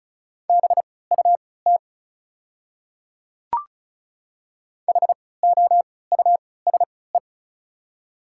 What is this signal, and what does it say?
Morse code, heard as a keyed tone, about 36 wpm, spaced as for 20 wpm, 690 Hz. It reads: BUT HOUSE
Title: 3x Character Spacing - Top 200 Words - 35wpm